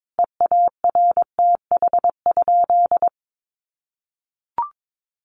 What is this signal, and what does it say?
Morse code, heard as a keyed tone, about 22 wpm, 700 Hz. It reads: EARTH?
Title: Sentences from Top 1200 Words 22wpm